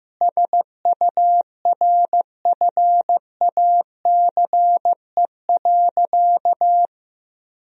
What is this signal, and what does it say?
Morse code, heard as a keyed tone, about 15 wpm, 700 Hz. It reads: SURFACE.
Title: Sentences from Top 1000 Words - Review Part 3 - 15wpm